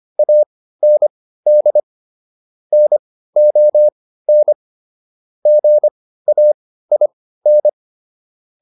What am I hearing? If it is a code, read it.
Morse code, heard as a keyed tone, about 25 wpm, spaced as for 15 wpm, 600 Hz. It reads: AND NON GAIN